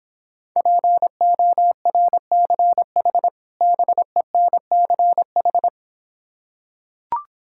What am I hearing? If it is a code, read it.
Morse code, heard as a keyed tone, about 26 wpm, 700 Hz. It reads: PORCH BENCH